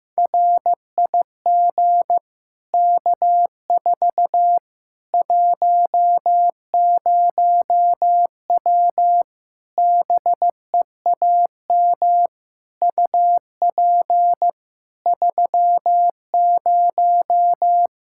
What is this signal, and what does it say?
Morse code, heard as a keyed tone, about 15 wpm, 700 Hz. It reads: RIG K4 10W BEAM UP 30
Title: K1USN SST QSO Practice 15wpm